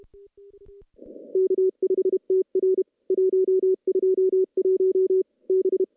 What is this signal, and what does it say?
Morse code, heard as a keyed tone, about 32 wpm, 380 Hz. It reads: K5TR 121 B
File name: contests / CW-SS-k5tr- -0000